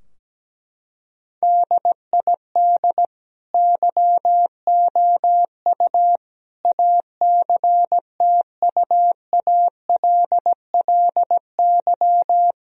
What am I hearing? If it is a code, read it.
Morse code, heard as a keyed tone, about 17 wpm, 700 Hz. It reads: DID YOU ACTUALLY
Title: Sentences from Top 1000 Words - Review Part 4 - 17wpm